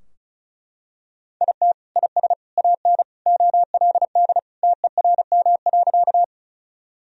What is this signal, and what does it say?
Morse code, heard as a keyed tone, about 36 wpm, 700 Hz. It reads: IT IS AN OLD TERM.
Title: Sentences from Top 1000 Words - Review Part 4 - 35wpm